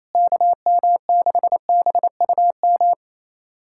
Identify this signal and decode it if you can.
Morse code, heard as a keyed tone, about 28 wpm, 700 Hz. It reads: KM6BUM
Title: USA Callsign DE Callsign 28wpm